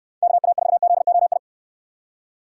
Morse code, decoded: KA4ZOI

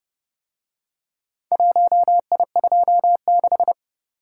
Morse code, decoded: 1I26

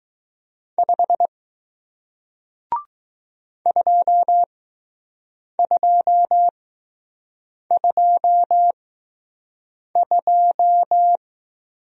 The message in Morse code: 5 2222

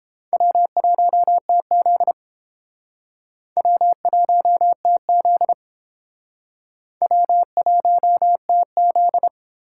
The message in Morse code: W1TZ W1TZ W1TZ

